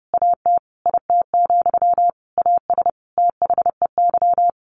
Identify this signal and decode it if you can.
Morse code, heard as a keyed tone, about 30 wpm, 700 Hz. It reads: AT IT, AS THEY
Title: A Sound Of Thunder 30wpm